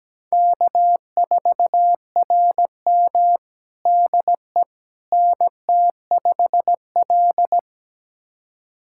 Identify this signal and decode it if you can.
Morse code, heard as a keyed tone, about 17 wpm, 700 Hz. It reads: K4RM DE NT5L
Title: USA Callsign DE Callsign - 4 Characters - 17wpm